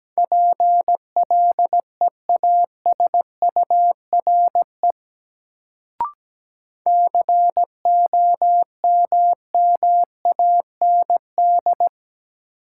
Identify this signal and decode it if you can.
Morse code, heard as a keyed tone, about 17 wpm, 700 Hz. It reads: PLEASURE COMMAND
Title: Sentences from Top 1600 Words - New Word Drill - 17wpm